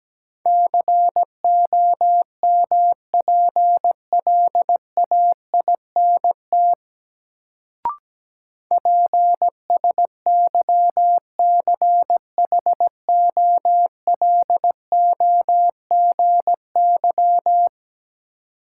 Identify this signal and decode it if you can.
Morse code, heard as a keyed tone, about 17 wpm, 700 Hz. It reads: COMPLAINT PSYCHOLOGY
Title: Sentences from Top 1400 Words - New Word Drill - 17wpm